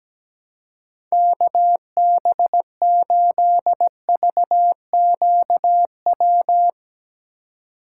Morse code, heard as a keyed tone, about 17 wpm, 700 Hz. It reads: KB8VQW
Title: USA Call Signs - 4x Character Spacing - 17wpm